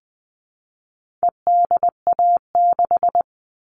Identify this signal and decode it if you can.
Morse code, heard as a keyed tone, about 20 wpm, 700 Hz. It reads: EDA6
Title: All Cut Numbers - 4-Digit 20wpm